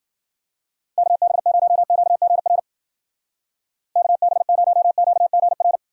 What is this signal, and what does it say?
Morse code, heard as a keyed tone, about 50 wpm, 700 Hz. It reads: KB0QGR KB0QGR
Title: Speed Racing - USA Call Signs 40wpm